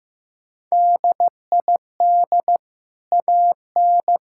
Morse code, heard as a keyed tone, about 15 wpm, 700 Hz. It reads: DID AN
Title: Sentences from Top 1000 Words - Review Part 6 - 15wpm